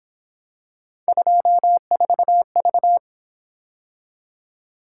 Morse code, heard as a keyed tone, about 28 wpm, 700 Hz. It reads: 24V